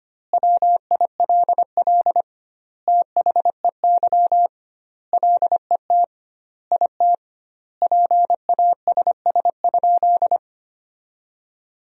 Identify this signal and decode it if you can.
Morse code, heard as a keyed tone, about 25 wpm, 700 Hz. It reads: WILL THEY LET IT PASS?
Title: 2x Word-Spacing - Sentences from Top 400 Words - 25wpm